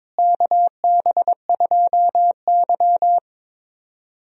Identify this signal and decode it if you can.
Morse code, heard as a keyed tone, about 22 wpm, 700 Hz. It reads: KB2Y
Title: USA Call Signs - 4 Characters - Repeated 3x - 22wpm